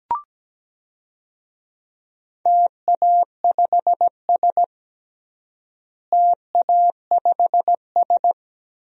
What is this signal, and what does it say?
Morse code, heard as a keyed tone, about 17 wpm, 700 Hz. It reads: TA5S TA5S
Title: DXCC-100 Call Signs - 4 Characters - Repeated 2x - 17wpm